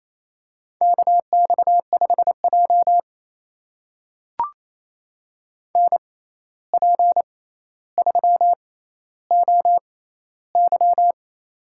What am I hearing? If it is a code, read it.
Morse code, heard as a keyed tone, about 28 wpm, spaced as for 13 wpm, 700 Hz. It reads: KX5J NP3OY
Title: USA Call Signs - 6x Character Spacing - 28wpm